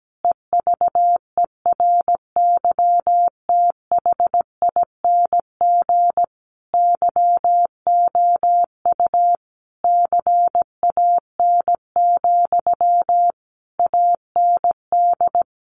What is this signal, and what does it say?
Morse code, heard as a keyed tone, about 17 wpm, 700 Hz. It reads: EVERYTHING YOU CAN, AND
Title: Sentences from Top 1300 Words 17wpm